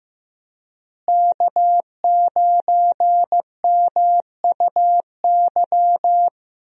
Morse code, heard as a keyed tone, about 16 wpm, 700 Hz. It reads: K9MUY
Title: USA Call Signs - 5 Characters - Repeated 3x - 15wpm